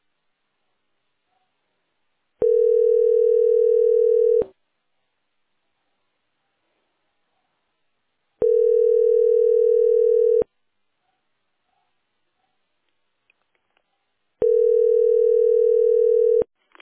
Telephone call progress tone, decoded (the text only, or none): ringback tone